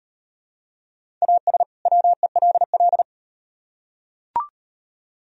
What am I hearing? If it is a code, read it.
Morse code, heard as a keyed tone, about 38 wpm, 700 Hz. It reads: AS WELL